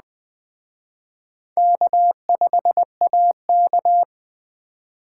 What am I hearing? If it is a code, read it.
Morse code, heard as a keyed tone, about 20 wpm, 700 Hz. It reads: K5AK